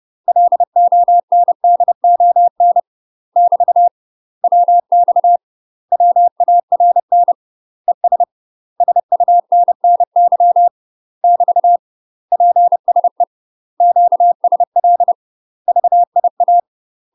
Morse code, heard as a keyed tone, about 30 wpm, 700 Hz. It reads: LONDON <BT> WX WARN ES SUNNY <BT> PSE QSL VIA
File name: UK MASTER QSOs TRAINING / 30WPM / UK MASTER QSOs 0044